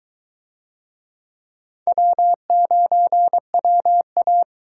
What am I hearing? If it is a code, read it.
Morse code, heard as a keyed tone, about 22 wpm, 700 Hz. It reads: W9WA